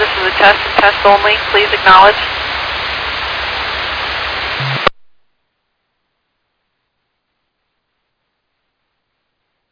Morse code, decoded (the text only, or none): none